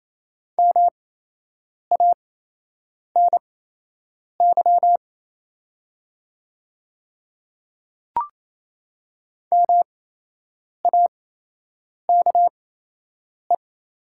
Morse code, heard as a keyed tone, about 28 wpm, spaced as for 8 wpm, 700 Hz. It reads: MANY MAKE
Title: 8x Character Spacing - Top 100 Words - 28wpm